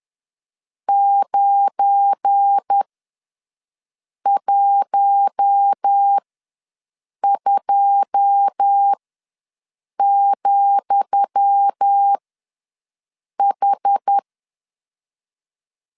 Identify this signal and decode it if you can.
Morse code, heard as a keyed tone, about 10 wpm, spaced as for 5 wpm, 800 Hz. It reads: 912,H
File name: LEZIONI / 10 SIGNS / 10D COMA ws 10wpm 4314